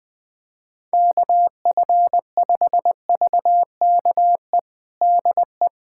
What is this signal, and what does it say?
Morse code, heard as a keyed tone, about 20 wpm, 700 Hz. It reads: KF5VKE DE
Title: USA Callsign DE Callsign - 6 Characters - 20wpm